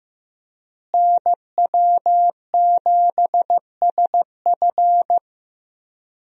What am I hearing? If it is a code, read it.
Morse code, heard as a keyed tone, about 15 wpm, 700 Hz. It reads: NW7SF
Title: USA Call Signs - 5 Characters - Repeated 3x - 15wpm